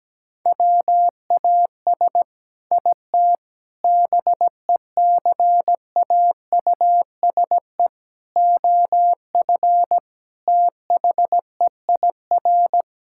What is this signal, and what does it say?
Morse code, heard as a keyed tone, about 17 wpm, 700 Hz. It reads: WAS IT BECAUSE OF THEIR
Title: Sentences from Top 1000 Words - Review Part 6 - 17wpm